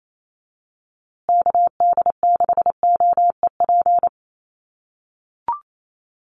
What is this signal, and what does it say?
Morse code, heard as a keyed tone, about 28 wpm, 700 Hz. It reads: KD6OEP
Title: USA Call Signs Repeated 2x 28wpm